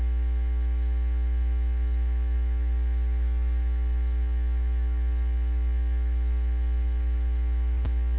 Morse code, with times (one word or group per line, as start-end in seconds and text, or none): none